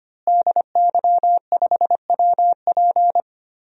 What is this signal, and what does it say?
Morse code, heard as a keyed tone, about 25 wpm, 700 Hz. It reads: DY5WP
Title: DXCC-100 Call Signs - 5 Characters - Repeated 3x - 25wpm